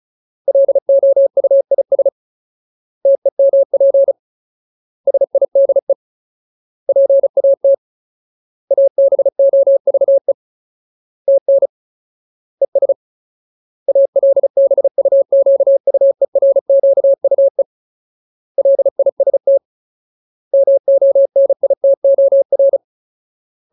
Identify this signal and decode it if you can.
Morse code, heard as a keyed tone, about 34 wpm, 550 Hz. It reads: LOUIS TEMP SIDE PAT ABOVE TN ES ALBUQUERQUE LIST MONITOR
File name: Ultimate Word List / 35WPM Ultimate